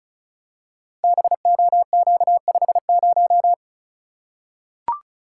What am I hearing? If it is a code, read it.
Morse code, heard as a keyed tone, about 36 wpm, 690 Hz. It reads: BOQ50
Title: Code Groups - 5 Letter:Numbers - 35wpm